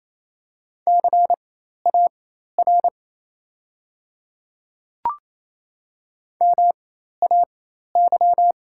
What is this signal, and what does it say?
Morse code, heard as a keyed tone, about 28 wpm, spaced as for 13 wpm, 700 Hz. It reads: CAR MAY
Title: 4x Character Spacing - Top 300 Words - 28wpm